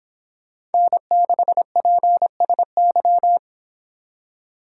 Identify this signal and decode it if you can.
Morse code, heard as a keyed tone, about 30 wpm, 700 Hz. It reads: N6PSY